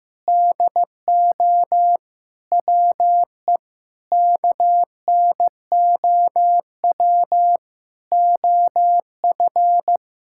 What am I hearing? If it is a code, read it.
Morse code, heard as a keyed tone, about 15 wpm, 700 Hz. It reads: DO WE KNOW OF
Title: Sentences from Top 1600 Words 15wpm